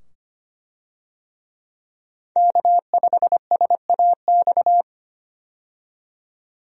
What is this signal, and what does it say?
Morse code, heard as a keyed tone, about 25 wpm, 700 Hz. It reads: K5SAX
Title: USA Call Signs - 5 Characters - 25wpm